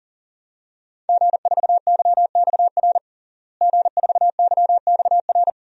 Morse code, heard as a keyed tone, about 40 wpm, 700 Hz. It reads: G4YXR G4YXR
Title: DXCC-100 Call Signs - 5 Characters - Repeated 2x - 40wpm